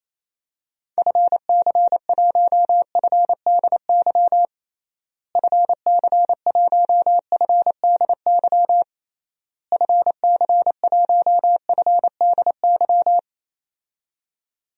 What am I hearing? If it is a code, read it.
Morse code, heard as a keyed tone, about 28 wpm, 700 Hz. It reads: FC1FDY FC1FDY FC1FDY
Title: DXCC-100 Call Signs - 6 Characters - Repeated 3x - 28wpm